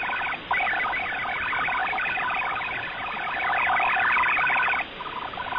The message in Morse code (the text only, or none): none